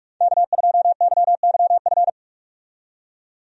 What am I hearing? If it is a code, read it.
Morse code, heard as a keyed tone, about 42 wpm, 700 Hz. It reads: K2YYF
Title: USA Call Signs - 5 Characters - 45wpm